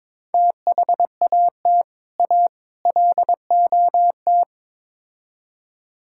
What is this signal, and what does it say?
Morse code, heard as a keyed tone, about 22 wpm, 700 Hz. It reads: THAT A LOT
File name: practice / 22wpm / Build A Phrase - 5-Word Phrases - 22wpm